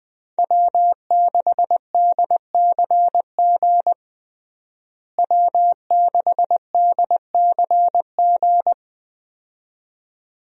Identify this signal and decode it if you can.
Morse code, heard as a keyed tone, about 20 wpm, 700 Hz. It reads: W6DCG W6DCG
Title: USA Call Signs - 5 Characters - Repeated 2x - 20wpm